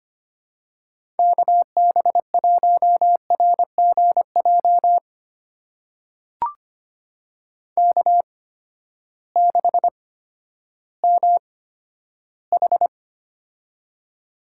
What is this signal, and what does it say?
Morse code, heard as a keyed tone, about 25 wpm, spaced as for 12 wpm, 700 Hz. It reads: KB1RGJ K6MH
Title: USA Call Signs - 8x Character Spacing - 25wpm